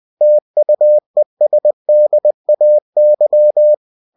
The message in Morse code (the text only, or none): TUESDAY